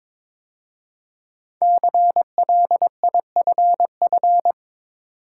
Morse code, heard as a keyed tone, about 22 wpm, 690 Hz. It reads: CLIFF
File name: 5 Letter Words - First 3 Letters Given - 22wpm